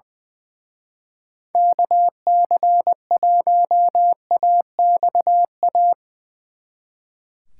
Morse code, heard as a keyed tone, about 24 wpm, 700 Hz. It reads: KC1AXA
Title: Speed Racing - USA 2x3 Call Signs 20wpm